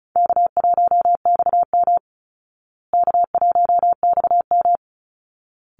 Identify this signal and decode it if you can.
Morse code, heard as a keyed tone, about 34 wpm, 700 Hz. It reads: K1XM K1XM
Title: Call Signs Repeated 3x - 35wpm